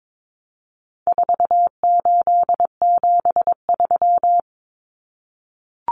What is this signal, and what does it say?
Morse code, heard as a keyed tone, about 22 wpm, 700 Hz. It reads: 4873
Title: Common Cut Numbers - 4-Digit 22wpm